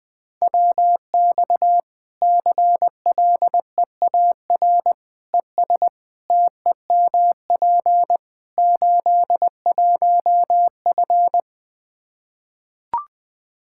Morse code, heard as a keyed tone, about 20 wpm, 700 Hz. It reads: WX CLEAR ES TEMP 81F
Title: Practice QSO - Exchange 2 - Info - 20wpm